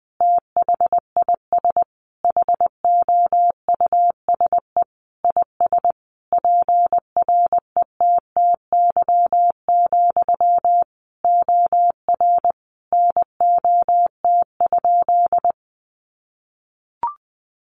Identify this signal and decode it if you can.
Morse code, heard as a keyed tone, about 19 wpm, 700 Hz. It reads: THIS HOUSE IS PRETTY, OR NOT?
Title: Sentences from Top 1000 Words 20wpm